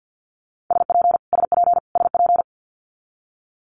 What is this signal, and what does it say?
Morse code, heard as a keyed tone, about 50 wpm, 700 Hz. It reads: SP SP SP
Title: Multi-letter Phonemes Repeated 3x 50wpm